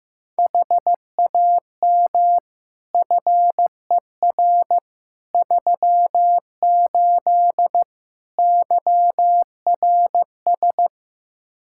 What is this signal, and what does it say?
Morse code, heard as a keyed tone, about 15 wpm, 700 Hz. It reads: HAM FER 38 YRS
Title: K1USN SST QSO Practice 15wpm